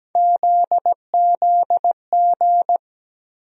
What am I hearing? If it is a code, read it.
Morse code, heard as a keyed tone, about 17 wpm, 700 Hz. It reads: ZZG
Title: USA Call Sign Postfixes - 3 Letters 17wpm